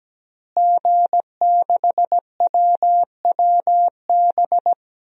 Morse code, heard as a keyed tone, about 17 wpm, 700 Hz. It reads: G6WWB
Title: DXCC-100 Call Signs - 5 Characters - Repeated 2x - 17wpm